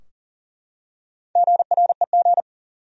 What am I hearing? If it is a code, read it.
Morse code, heard as a keyed tone, about 40 wpm, 700 Hz. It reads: GREG
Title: USA Common Names - Max 5 Letters - 40wpm